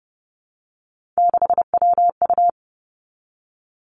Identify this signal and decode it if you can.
Morse code, heard as a keyed tone, about 30 wpm, 700 Hz. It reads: 6WU
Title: Three Challenging Letter-Number Combinations 30wpm